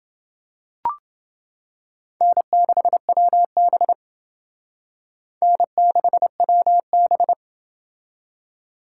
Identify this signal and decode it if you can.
Morse code, heard as a keyed tone, about 29 wpm, 700 Hz. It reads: N6WB N6WB